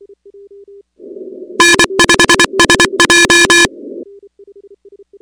Morse code, heard as a keyed tone, about 24 wpm, 1700 Hz, with noise. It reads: N5SJ
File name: contests / 2019-cw-wpx-kt5j-k5pi- -2330 0-13